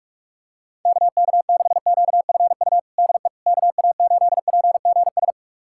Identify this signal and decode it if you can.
Morse code, heard as a keyed tone, about 46 wpm, 700 Hz. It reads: KK6QFU DE KA8PGS
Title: USA Callsign DE Callsign - 6 Characters - 45wpm